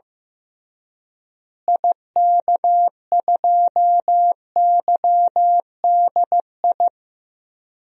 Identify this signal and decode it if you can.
Morse code, heard as a keyed tone, about 15 wpm, 700 Hz. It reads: IK2YDI
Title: DXCC-100 Call Signs - 6 Characters - Repeated 3x - 15wpm